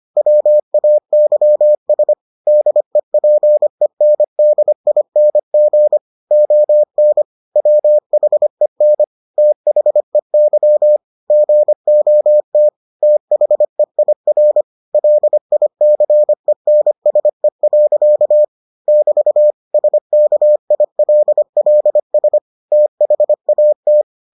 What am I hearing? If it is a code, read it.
Morse code, heard as a keyed tone, about 25 wpm, 600 Hz. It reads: WAYS DEPENDING ON WHEN THEY GOT THEIR LICENSE. <BT> SKILLS THAT